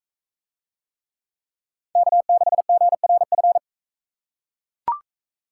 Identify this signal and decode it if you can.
Morse code, heard as a keyed tone, about 40 wpm, 700 Hz. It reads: K6GRF